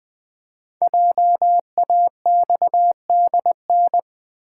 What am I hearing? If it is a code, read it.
Morse code, heard as a keyed tone, about 20 wpm, 700 Hz. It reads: JAXDN
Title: Code Groups - 5 Letters - 20wpm